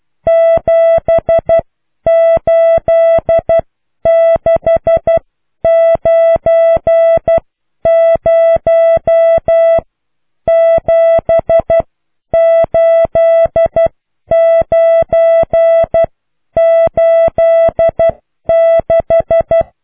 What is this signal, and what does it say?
Morse code, heard as a keyed tone, about 12 wpm, spaced as for 9 wpm, 660 Hz. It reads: 7869078986